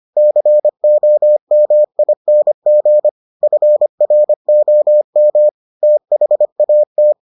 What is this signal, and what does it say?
Morse code, heard as a keyed tone, about 25 wpm, 600 Hz. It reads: COMING FROM THAT